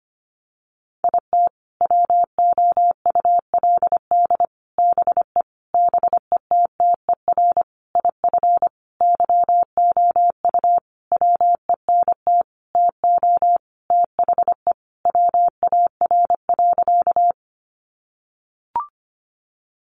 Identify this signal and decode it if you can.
Morse code, heard as a keyed tone, about 25 wpm, 700 Hz. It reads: IT WOULD BE BETTER IF YOU WENT TO THE WAR.
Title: Sentences from Top 400 Words 25wpm